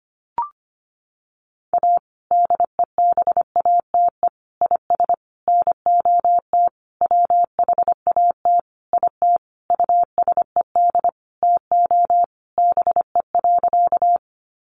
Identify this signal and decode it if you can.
Morse code, heard as a keyed tone, about 25 wpm, 700 Hz. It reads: A DEBATE IS NOT WHAT IT USED TO BE.
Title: Sentences from Top 1100 Words 25wpm